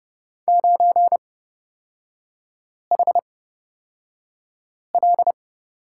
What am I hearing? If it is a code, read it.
Morse code, heard as a keyed tone, about 31 wpm, spaced as for 5 wpm, 700 Hz. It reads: 9HL
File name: Lesson 38e - Letter q - Three Characters 30wpm